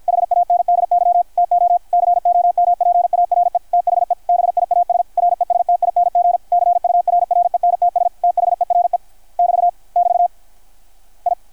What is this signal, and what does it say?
Morse code, heard as a keyed tone, about 51 wpm, 660 Hz. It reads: HANDY TO COMPARE THE BIAS RESTING CURRENTS THERE <BT> <BT> I
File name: archived / CRSnet